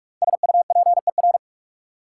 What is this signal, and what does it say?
Morse code, heard as a keyed tone, about 46 wpm, 700 Hz. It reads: SUPER
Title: Five Letter Words - 45wpm